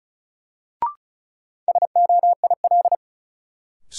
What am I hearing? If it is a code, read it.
Morse code, heard as a keyed tone, about 36 wpm, 700 Hz. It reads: SOIL
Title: Sentences from Top 1000 Words - Review Part 4 - New Word Drill - 35wpm